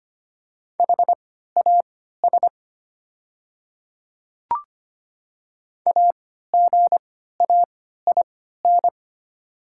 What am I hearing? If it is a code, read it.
Morse code, heard as a keyed tone, about 25 wpm, spaced as for 14 wpm, 700 Hz. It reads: HAS AGAIN
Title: 3x Character Spacing - Top 200 Words - 25wpm